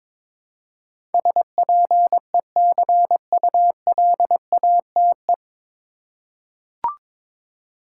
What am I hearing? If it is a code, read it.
Morse code, heard as a keyed tone, about 22 wpm, 700 Hz. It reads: SPECULATE
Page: Morse Code Ninja practice 22wpm